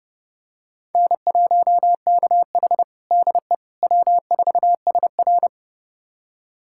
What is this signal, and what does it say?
Morse code, heard as a keyed tone, about 32 wpm, 700 Hz. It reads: N1KH DE W4SR
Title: USA Callsign DE Callsign - 4 Characters - 30wpm